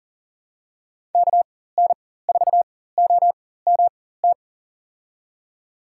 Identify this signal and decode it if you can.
Morse code, heard as a keyed tone, about 38 wpm, spaced as for 18 wpm, 700 Hz. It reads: KN4OMT
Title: USA Call Signs - 6 Characters - 4x Character Spacing - 40wpm